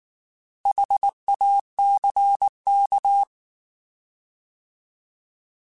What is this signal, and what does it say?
Morse code, heard as a keyed tone, about 19 wpm, 800 Hz. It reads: HACK